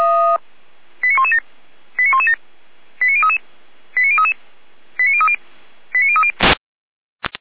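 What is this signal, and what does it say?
Five-tone selective call (ZVEI), heard as quick runs of five tones, tones of about 80 ms. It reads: 89117 89117 89033 89033 89030 89030
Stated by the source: ZVEI five-tone selective call received with an SDR.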